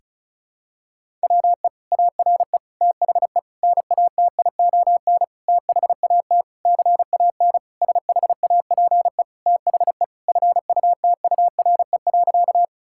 Morse code, heard as a keyed tone, about 36 wpm, 700 Hz. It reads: WE ARE THE NATION THAT CAN SHAPE THE FUTURE.
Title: Sentences from Top 1000 Words - Review Part 6 - 35wpm